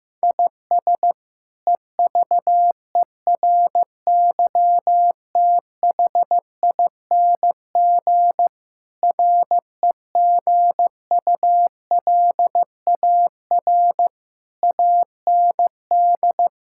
Morse code, heard as a keyed tone, about 15 wpm, 700 Hz. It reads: IS EVERYTHING REGULAR AND
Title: Sentences from Top 1000 Words - Review Part 6 - 15wpm